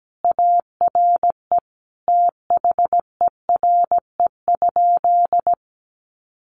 Morse code, heard as a keyed tone, about 17 wpm, 700 Hz. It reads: ARE THERE?
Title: Sentences from Top 100 Words 17wpm